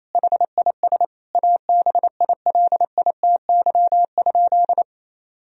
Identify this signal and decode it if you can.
Morse code, heard as a keyed tone, about 28 wpm, 700 Hz. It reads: HIS ABILITY?
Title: Sentences from Top 1000 Words - Review Part 4 - 28wpm